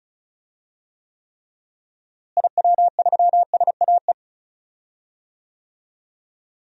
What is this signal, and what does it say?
Morse code, heard as a keyed tone, about 36 wpm, 700 Hz. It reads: IW3SAE
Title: DXCC-100 Call Signs - 6 Characters - Repeated 3x - 35wpm